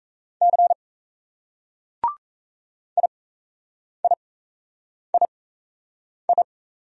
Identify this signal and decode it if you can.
Morse code, heard as a keyed tone, about 35 wpm, spaced as for 8 wpm, 700 Hz. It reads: C IIII